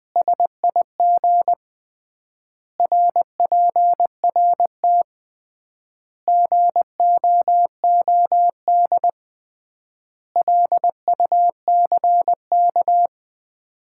Morse code, heard as a keyed tone, about 20 wpm, 700 Hz. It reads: SIG RPRT GOOD LUCK
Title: Common QSO Phrases - 3x Word-Spacing - 20wpm